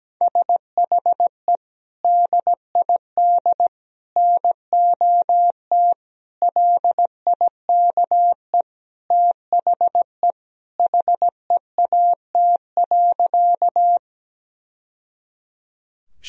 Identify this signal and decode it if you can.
Morse code, heard as a keyed tone, about 17 wpm, 700 Hz. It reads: SHE DID NOT LIKE THE HEAT.
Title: Sentences from Top 1000 Words - Review Part 3 - 17wpm